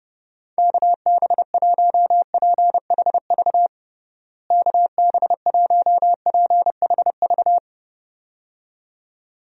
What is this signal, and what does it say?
Morse code, heard as a keyed tone, about 30 wpm, 700 Hz. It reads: KB1PHV KB1PHV
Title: USA Call Signs - 6 Characters - Repeated 2x - 30wpm